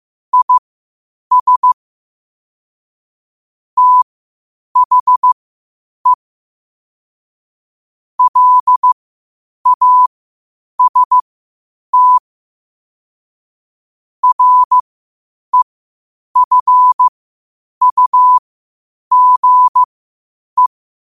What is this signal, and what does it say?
Morse code, heard as a keyed tone, about 15 wpm, spaced as for 8 wpm, 1000 Hz. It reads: IS THE LAST REFUGE
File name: qotdfiles / 8wpm